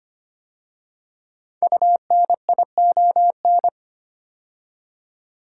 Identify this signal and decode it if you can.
Morse code, heard as a keyed tone, about 25 wpm, 700 Hz. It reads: UNION
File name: 5 Letter Words - First 3 Letters Given - 25wpm